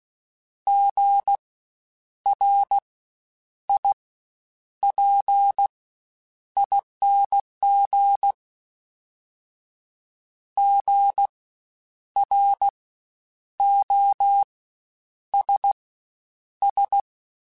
Morse code, heard as a keyed tone, about 16 wpm, spaced as for 8 wpm, 800 Hz. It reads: GRIPING GROSS
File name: Spaced Alphas / Spaced Alphas mp3s / Gs sp @15